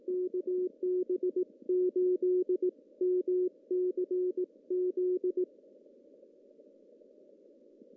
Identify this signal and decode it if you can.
Morse code, heard as a keyed tone, about 18 wpm, 1100 Hz. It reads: KB8MCZ
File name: CW / cqww-cw-k5tr